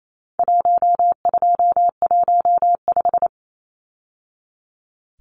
Morse code, heard as a keyed tone, about 28 wpm, 700 Hz. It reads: 1215
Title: Numbers - 4-Digit 28wpm